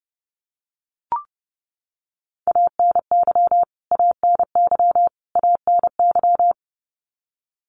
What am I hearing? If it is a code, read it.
Morse code, heard as a keyed tone, about 30 wpm, 700 Hz. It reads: ANY ANY ANY
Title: Top 200 Words Repeated 30wpm